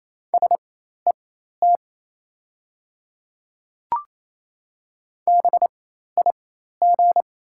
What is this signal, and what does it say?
Morse code, heard as a keyed tone, about 28 wpm, spaced as for 13 wpm, 700 Hz. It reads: SET BIG
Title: 4x Character Spacing - Top 200 Words - 28wpm